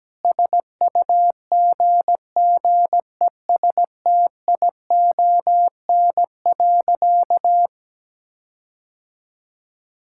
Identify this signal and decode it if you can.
Morse code, heard as a keyed tone, about 17 wpm, 700 Hz. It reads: SUGGESTION.
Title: Sentences from Top 1600 Words 17wpm